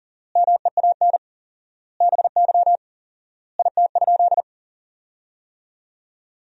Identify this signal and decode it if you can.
Morse code, heard as a keyed tone, about 40 wpm, 700 Hz. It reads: MEAN BY IT?
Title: 4x Word-Spacing - Sentences from Top 200 Words - 40wpm